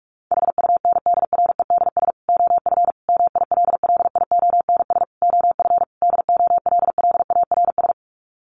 Morse code, heard as a keyed tone, about 44 wpm, 700 Hz. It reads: HUNDREDS OF MILLIONS OF DOLLARS
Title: Sets of 5 Words 45wpm